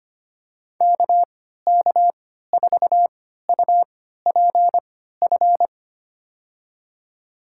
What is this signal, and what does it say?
Morse code, heard as a keyed tone, about 25 wpm, spaced as for 14 wpm, 700 Hz. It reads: KK4UPF
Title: USA Call Signs - 6 Characters - 3x Character Spacing - 25wpm